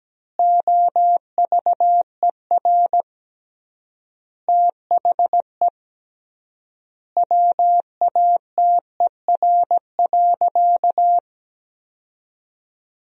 Morse code, heard as a keyed tone, about 17 wpm, 700 Hz. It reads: OVER THE WATER.
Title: 3x Word-Spacing - Sentences from Top 300 Words - 17wpm